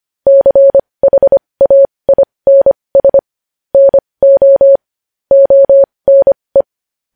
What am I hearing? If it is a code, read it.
Morse code, heard as a keyed tone, about 25 wpm, spaced as for 20 wpm, 550 Hz. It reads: CHAINS NO ONE
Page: news-headlines-in-morse-code-at-20-wpm